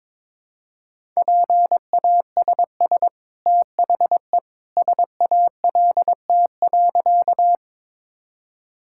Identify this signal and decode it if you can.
Morse code, heard as a keyed tone, about 22 wpm, 700 Hz. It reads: PASS THE SALT.
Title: Sentences from Top 1000 Words - Review Part 5 - 22wpm